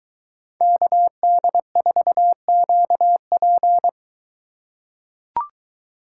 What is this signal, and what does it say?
Morse code, heard as a keyed tone, about 23 wpm, 700 Hz. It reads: KD4QP